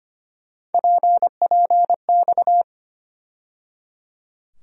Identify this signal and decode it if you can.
Morse code, heard as a keyed tone, about 25 wpm, 700 Hz. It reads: PPX
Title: USA Call Sign Postfixes - 3 Letters 25wpm